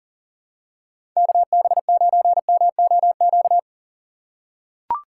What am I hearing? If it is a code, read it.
Morse code, heard as a keyed tone, about 40 wpm, 700 Hz. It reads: KB9MOQ
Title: USA Call Signs - 6 Characters - 3x Character Spacing - 40wpm